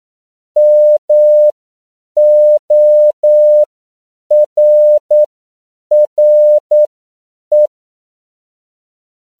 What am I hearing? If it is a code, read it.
Morse code, heard as a keyed tone, about 9 wpm, spaced as for 7 wpm, 600 Hz. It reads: MORRE
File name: CW / 10PPM